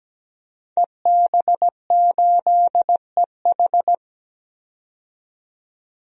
Morse code, heard as a keyed tone, about 17 wpm, 700 Hz. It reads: EB8EH
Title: DXCC-100 Call Signs - 5 Characters - 17wpm